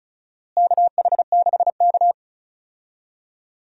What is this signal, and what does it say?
Morse code, heard as a keyed tone, about 34 wpm, 700 Hz. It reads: KH6K
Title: DXCC-100 Call Signs - 4 Characters - Repeated 2x - 35wpm